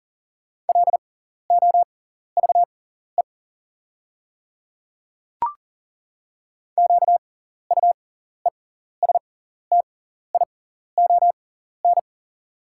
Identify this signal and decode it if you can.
Morse code, heard as a keyed tone, about 40 wpm, spaced as for 14 wpm, 690 Hz. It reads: LOVE QUESTION
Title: 6x Character Spacing - Top 500 Words - 40wpm